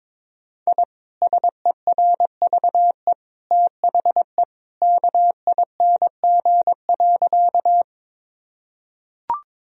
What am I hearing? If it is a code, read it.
Morse code, heard as a keyed tone, about 22 wpm, 700 Hz. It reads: I SERVE THE KING.
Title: Sentences from Top 1000 Words - Review Part 2 - 22wpm